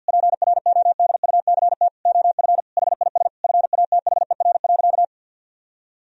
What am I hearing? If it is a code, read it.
Morse code, heard as a keyed tone, about 50 wpm, 700 Hz. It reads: PRODUCT OF HIS FATHER.
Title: Sentences from Top 1000 Words - Review Part 2 - 50wpm